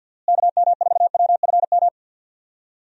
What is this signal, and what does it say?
Morse code, heard as a keyed tone, about 50 wpm, 700 Hz. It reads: KM4WFM
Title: USA Call Signs - 6 Characters - Repeated 3x - 50wpm